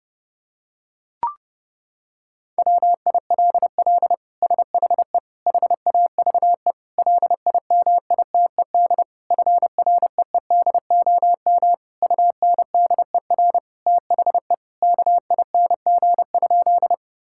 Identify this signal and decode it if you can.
Morse code, heard as a keyed tone, about 30 wpm, 700 Hz. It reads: WILL SHE HAVE LIMITED FREEDOM UNDER THE KING?
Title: Sentences from Top 1000 Words - Review Part 6 - 30wpm